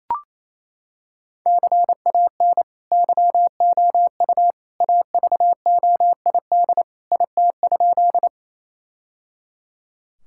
Morse code, heard as a keyed tone, about 27 wpm, 700 Hz. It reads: CAN YOU AVOID IT?
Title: Sentences from Top 1000 Words - Review Part 6 - 28wpm